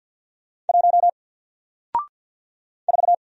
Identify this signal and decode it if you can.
Morse code, heard as a keyed tone, about 54 wpm, 700 Hz. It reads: 1 4
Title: Single Letter-Number Alt 50wpm